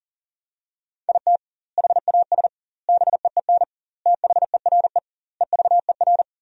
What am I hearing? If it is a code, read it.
Morse code, heard as a keyed tone, about 40 wpm, 700 Hz. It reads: IT HAS BEEN THERE EVER
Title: 2x Word-Spacing - Sentences from Top 400 Words - 40wpm